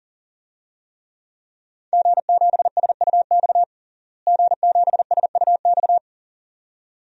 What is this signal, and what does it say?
Morse code, heard as a keyed tone, about 40 wpm, 700 Hz. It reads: G7SUX G7SUX
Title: DXCC-100 Call Signs - 5 Characters - Repeated 2x - 40wpm